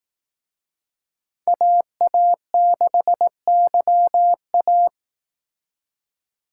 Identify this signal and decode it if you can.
Morse code, heard as a keyed tone, about 18 wpm, 700 Hz. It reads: AA6YA
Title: Speed Racing - USA 2x2 Call Signs 15wpm